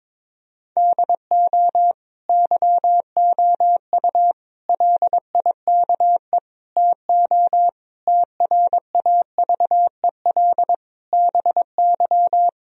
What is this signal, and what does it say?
Morse code, heard as a keyed tone, about 22 wpm, 700 Hz. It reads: DO YOU LIKE TO TRAVEL BY
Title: Sentences from Top 1000 Words - Review Part 3 - 22wpm